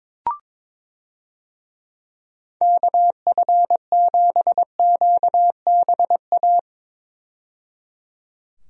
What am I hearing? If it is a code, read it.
Morse code, heard as a keyed tone, about 22 wpm, 700 Hz. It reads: KF7QBA
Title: USA Call Signs - 6 Characters - 22wpm